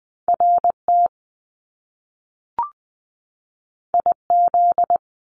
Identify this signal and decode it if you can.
Morse code, heard as a keyed tone, about 20 wpm, 700 Hz. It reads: RT IZ
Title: Top Two Letter Combinations in Words 20wpm